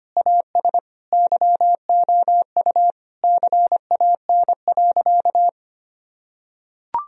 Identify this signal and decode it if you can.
Morse code, heard as a keyed tone, about 25 wpm, 700 Hz. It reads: AS YOU CAN.
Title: Sentences from Top 1000 Words - Review Part 3 - 25wpm